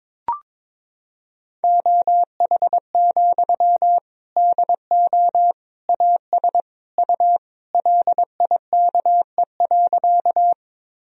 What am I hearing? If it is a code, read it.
Morse code, heard as a keyed tone, about 21 wpm, 700 Hz. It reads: OH, DO AS U LIKE.
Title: Sentences from Top 500 Words and Abbreviations 22wpm